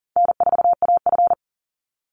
Morse code, heard as a keyed tone, about 40 wpm, 700 Hz. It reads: N4AF
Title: Easy Call Signs - 40wpm